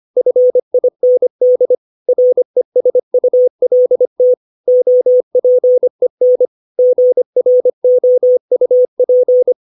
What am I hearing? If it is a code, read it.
Morse code, heard as a keyed tone, about 25 wpm, 500 Hz. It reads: FIND RESULT OPEN GROUP
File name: 632 words / 632 Words 25WPM